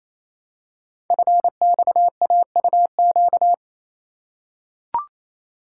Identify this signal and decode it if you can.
Morse code, heard as a keyed tone, about 28 wpm, 700 Hz. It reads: FXAUQ
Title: Code Groups - 5 Letters - 28wpm